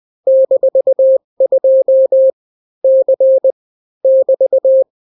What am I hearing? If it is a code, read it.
Morse code, heard as a keyed tone, about 20 wpm, 530 Hz. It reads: -2 C <BT>
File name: daily / 18wpm